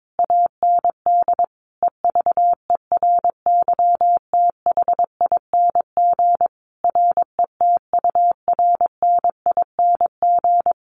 Morse code, heard as a keyed tone, about 22 wpm, 700 Hz. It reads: AND EVERYTHING RETURNING
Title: A Sound Of Thunder 22wpm